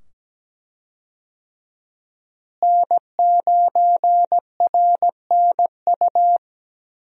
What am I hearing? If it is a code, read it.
Morse code, heard as a keyed tone, about 17 wpm, 700 Hz. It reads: N9RNU
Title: USA Call Signs - 5 Characters - Repeated 2x - 17wpm